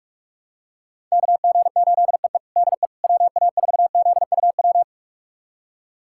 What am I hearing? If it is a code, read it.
Morse code, heard as a keyed tone, about 46 wpm, 700 Hz. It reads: KG8EE DE WA4ZUW
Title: USA Callsign DE Callsign 45wpm